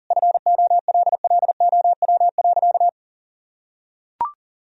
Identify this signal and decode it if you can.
Morse code, heard as a keyed tone, about 40 wpm, 700 Hz. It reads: FOLLOW.
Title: Sentences from Top 1000 Words - Review Part 3 - 40wpm